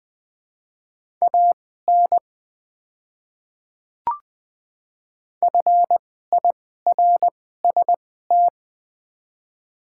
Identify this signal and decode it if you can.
Morse code, heard as a keyed tone, about 21 wpm, spaced as for 15 wpm, 700 Hz. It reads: AN FIRST
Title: 2x Character Spacing - Top 200 Words - 20wpm